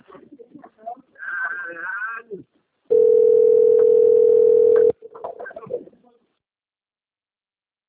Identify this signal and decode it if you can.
Ringback tone (North America), 440 + 480 Hz, with speech.
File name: rg-1600-unknown- - - .15599